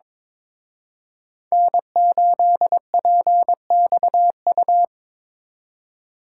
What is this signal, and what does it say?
Morse code, heard as a keyed tone, about 26 wpm, 700 Hz. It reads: N8PXU
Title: Speed Racing - USA 1x3 Call Signs 22wpm